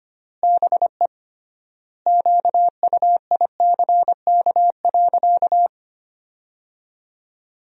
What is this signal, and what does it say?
Morse code, heard as a keyed tone, about 25 wpm, 700 Hz. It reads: BE QUICK.
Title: 3x Word-Spacing - Sentences from Top 500 Words - 25wpm